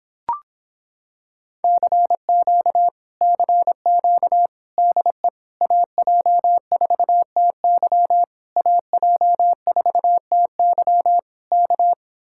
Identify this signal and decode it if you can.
Morse code, heard as a keyed tone, about 26 wpm, 700 Hz. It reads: CQ CQ DE AJ4TY AJ4TY K